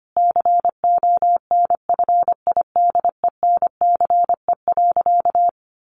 Morse code, heard as a keyed tone, about 25 wpm, 700 Hz. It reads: CONFIDENCE.
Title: Sentences from Top 1500 Words 25wpm